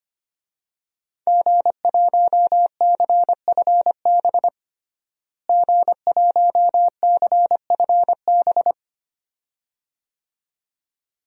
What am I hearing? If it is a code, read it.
Morse code, heard as a keyed tone, about 25 wpm, 700 Hz. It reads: G1CFB G1CFB
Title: DXCC-100 Call Signs - 5 Characters - Repeated 2x - 25wpm